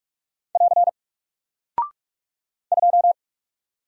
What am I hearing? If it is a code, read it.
Morse code, heard as a keyed tone, about 48 wpm, 700 Hz. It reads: <AR> 2
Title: Single Character (Letter:Number:Punctuation:Prosign) Alt 45wpm